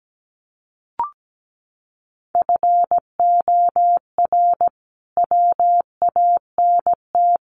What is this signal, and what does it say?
Morse code, heard as a keyed tone, about 17 wpm, 700 Hz. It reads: FOR WANT
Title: A Sound Of Thunder 17wpm